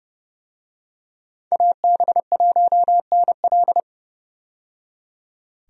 Morse code, heard as a keyed tone, about 30 wpm, 700 Hz. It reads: AB1NL
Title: USA Call Signs - 5 Characters - 30wpm